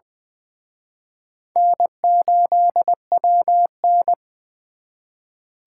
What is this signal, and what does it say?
Morse code, heard as a keyed tone, about 24 wpm, 700 Hz. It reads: N8WN
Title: Speed Racing - USA 1x2 Call Signs 20wpm